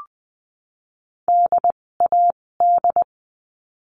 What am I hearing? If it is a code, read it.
Morse code, heard as a keyed tone, about 20 wpm, spaced as for 16 wpm, 700 Hz. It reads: DAD
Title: 256 Lesson 42a - Words 12-20wpm